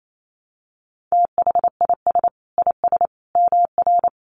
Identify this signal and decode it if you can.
Morse code, heard as a keyed tone, about 28 wpm, 700 Hz. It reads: THIS IS MR
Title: A Sound Of Thunder 28wpm